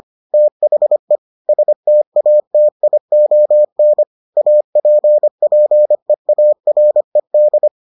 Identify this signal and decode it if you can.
Morse code, heard as a keyed tone, about 25 wpm, 600 Hz. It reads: THE STATION APPEARED